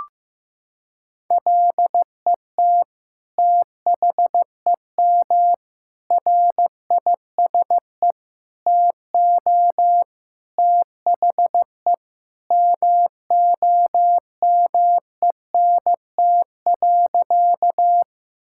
Morse code, heard as a keyed tone, about 15 wpm, 700 Hz. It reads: LET THEM RISE TO THE MOMENT.
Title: Sentences from Top 1000 Words - Review Part 6 - 15wpm